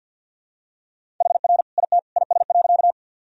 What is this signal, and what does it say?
Morse code, heard as a keyed tone, about 51 wpm, 700 Hz. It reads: HR IT IS.
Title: Sentences from Top 200 Words and Abbreviations 50wpm